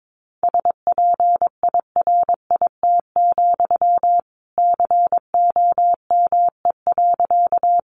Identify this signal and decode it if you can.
Morse code, heard as a keyed tone, about 22 wpm, 700 Hz. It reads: SPIRIT, COME.
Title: Sentences from Top 1400 Words 22wpm